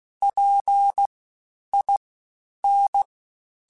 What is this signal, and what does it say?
Morse code, heard as a keyed tone, about 16 wpm, 800 Hz. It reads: P I N